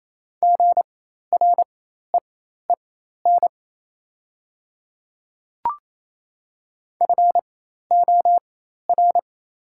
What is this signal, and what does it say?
Morse code, heard as a keyed tone, about 28 wpm, spaced as for 13 wpm, 700 Hz. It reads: GREEN FOR
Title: 4x Character Spacing - Top 500 Words - 28wpm